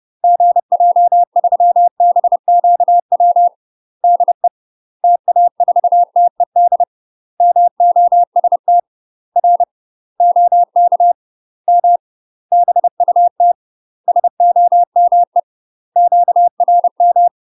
Morse code, heard as a keyed tone, about 30 wpm, 700 Hz. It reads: GJ3BQW DE TA4TED MOST R OK M BUT SOME QRM
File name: UK MASTER QSOs TRAINING / 30WPM / UK MASTER QSOs 0017